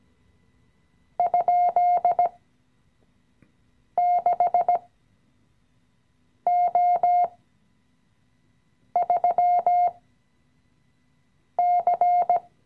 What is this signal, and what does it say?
Morse code, heard as a keyed tone, about 16 wpm, spaced as for 5 wpm, 710 Hz. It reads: ?6O3C
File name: CW-Online-Kurs / Anfänger / Lektion 018